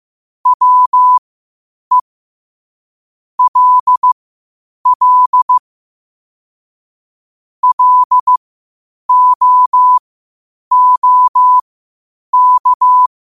Morse code, heard as a keyed tone, about 15 wpm, spaced as for 8 wpm, 1000 Hz. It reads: WE LL LOOK
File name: qotdfiles / 8wpm